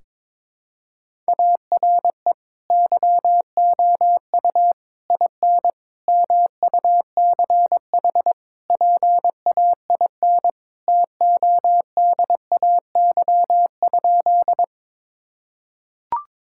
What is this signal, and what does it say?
Morse code, heard as a keyed tone, about 22 wpm, 700 Hz. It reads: ARE YOU IN MUCH PAIN TODAY?
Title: Sentences from Top 1600 Words 22wpm